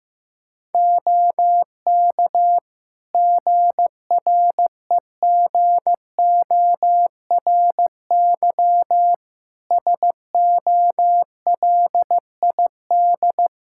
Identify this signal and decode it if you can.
Morse code, heard as a keyed tone, about 16 wpm, 700 Hz. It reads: OK GREGORY SOLID
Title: Practice QSO - Exchange 3 - More Info - 15wpm